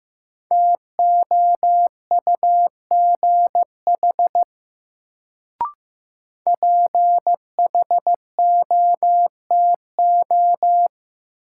Morse code, heard as a keyed tone, about 15 wpm, 700 Hz. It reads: TOUGH PHOTO
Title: Sentences from Top 1200 Words - New Word Drill Alt - 15wpm